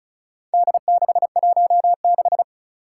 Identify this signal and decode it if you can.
Morse code, heard as a keyed tone, about 36 wpm, 700 Hz. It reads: D616
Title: Four Challenging Letter-Number Combinations 35wpm